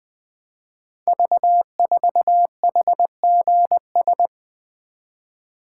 Morse code, heard as a keyed tone, about 20 wpm, 700 Hz. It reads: V4HGS
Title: Code Groups - 5 Letter:Numbers - 20wpm